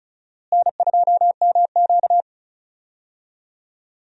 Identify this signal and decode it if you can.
Morse code, heard as a keyed tone, about 36 wpm, 700 Hz. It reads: N2MQ
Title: USA Call Signs - 4 Characters - Repeated 3x - 35wpm